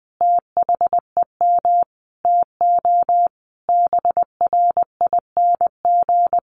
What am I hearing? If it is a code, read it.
Morse code, heard as a keyed tone, about 20 wpm, 700 Hz. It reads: THEM TO BRING
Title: Sentences from Top 1200 Words 20wpm